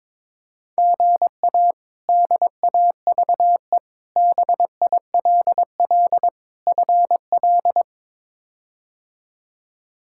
Morse code, heard as a keyed tone, about 22 wpm, 700 Hz. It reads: GA DAVE BILL FL
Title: K1USN SST QSO Practice - 22wpm